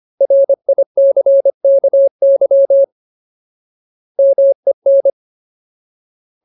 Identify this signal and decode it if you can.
Morse code, heard as a keyed tone, about 25 wpm, 550 Hz. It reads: RICKY MEN
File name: Ultimate Word List / 25WPM Ultimate